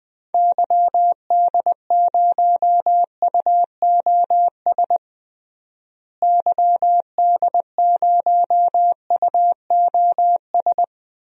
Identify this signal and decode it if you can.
Morse code, heard as a keyed tone, about 21 wpm, 700 Hz. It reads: YD0UOS YD0UOS
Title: DXCC-100 Call Signs - 6 Characters - Repeated 2x - 20wpm